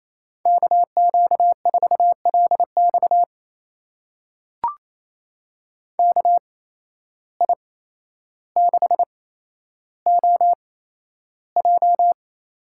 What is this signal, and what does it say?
Morse code, heard as a keyed tone, about 28 wpm, spaced as for 12 wpm, 700 Hz. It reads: KQ4LX KI6OJ
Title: USA Call Signs - 5 Characters - 8x Character Spacing - 28wpm